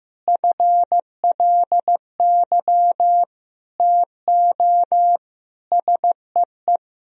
Morse code, heard as a keyed tone, about 15 wpm, 700 Hz. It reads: FLY TO SEE
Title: Sentences from Top 1000 Words - Review Part 2 - 15wpm